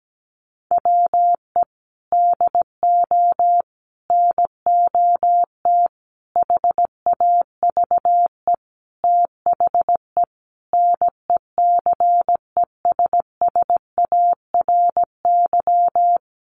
Morse code, heard as a keyed tone, about 17 wpm, 700 Hz. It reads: WE DO NOT HAVE THE NECESSARY
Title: Sentences from Top 1200 Words 17wpm